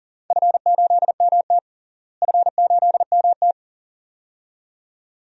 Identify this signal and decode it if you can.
Morse code, heard as a keyed tone, about 40 wpm, 700 Hz. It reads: F8MT F8MT
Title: DXCC-100 Call Signs - 4 Characters - Repeated 2x - 40wpm